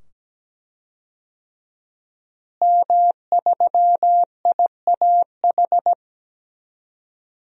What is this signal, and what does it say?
Morse code, heard as a keyed tone, about 17 wpm, 700 Hz. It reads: M3IAH
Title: DXCC-100 Call Signs - 5 Characters - Repeated 2x - 17wpm